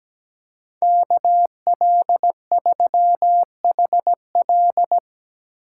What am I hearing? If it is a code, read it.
Morse code, heard as a keyed tone, about 19 wpm, 700 Hz. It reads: KL3HL
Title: Speed Racing - USA 2x2 Call Signs 17wpm